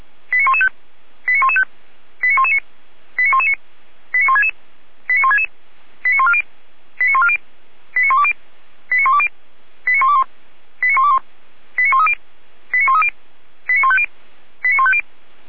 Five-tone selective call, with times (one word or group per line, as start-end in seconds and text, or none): 0.3-0.7 89115
1.3-1.6 89115
2.2-2.6 89119
3.2-3.5 89119
4.1-4.5 89166
5.1-5.5 89166
6.0-6.4 89140
7.0-7.4 89140
8.0-8.3 89120
8.9-9.3 89120
9.9-10.2 89121
10.8-11.2 89121
11.8-12.1 89130
12.7-13.1 89130
13.7-14.0 89150
14.6-15.0 89150